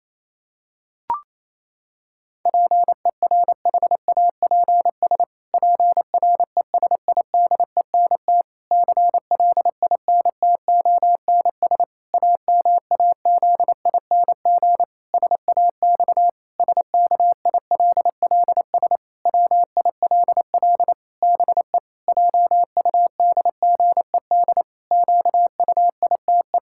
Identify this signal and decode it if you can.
Morse code, heard as a keyed tone, about 28 wpm, 700 Hz. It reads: PERHAPS PRESIDENT CLINTONS AMAZING SAX SKILLS WILL BE JUDGED QUITE
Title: Pangram Challenge v2 - 28wpm